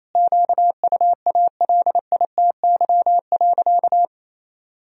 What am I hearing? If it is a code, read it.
Morse code, heard as a keyed tone, about 28 wpm, 700 Hz. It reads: QUALITY.
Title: Sentences from Top 1000 Words - Review Part 4 - 28wpm